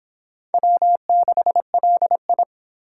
Morse code, heard as a keyed tone, about 25 wpm, 700 Hz. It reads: W6LI